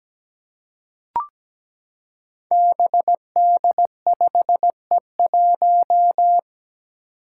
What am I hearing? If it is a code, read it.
Morse code, heard as a keyed tone, about 17 wpm, 700 Hz. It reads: BD5E1
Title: Five Challenging Letter-Number Combinations 17wpm